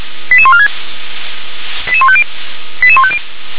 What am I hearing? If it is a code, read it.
Five-tone selective call (ZVEI), heard as quick runs of five tones, tones of about 70 ms. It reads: 88146 88150 88150
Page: SDR/ZVEI